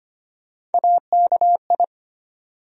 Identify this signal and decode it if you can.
Morse code, heard as a keyed tone, about 25 wpm, 700 Hz. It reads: AKI
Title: USA Call Sign Postfixes - 3 Letters 25wpm